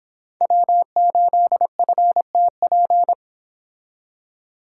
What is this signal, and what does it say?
Morse code, heard as a keyed tone, about 31 wpm, 700 Hz. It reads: W8FTP